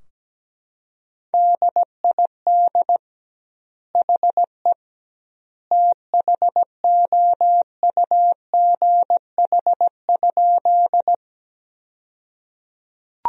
Morse code, heard as a keyed tone, about 17 wpm, 700 Hz. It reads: DID HE THOUGH?
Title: 2x Word-Spacing - Sentences from Top 400 Words - 17wpm